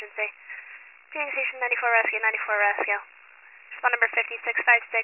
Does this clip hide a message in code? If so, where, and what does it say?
Yes: Emergency Alert System (SAME) tones from 0.4 to 4.3 s, heard as harsh buzzing bursts, with speech.